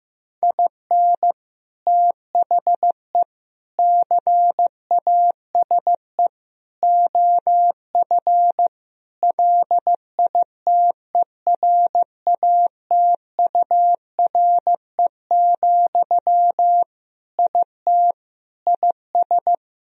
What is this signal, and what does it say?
Morse code, heard as a keyed tone, about 15 wpm, 700 Hz. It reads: IN THE CASE OF LITERATURE, IT IS
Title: Sentences from Top 1000 Words - Review Part 3 - 15wpm